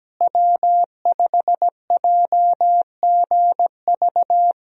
Morse code, heard as a keyed tone, about 17 wpm, 700 Hz. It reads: W5JGV
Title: USA Call Signs - 5 Characters - Repeated 2x - 17wpm